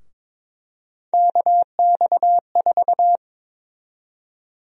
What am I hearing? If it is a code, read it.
Morse code, heard as a keyed tone, about 22 wpm, 700 Hz. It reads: KX4
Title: USA Call Sign Prefixes - 22wpm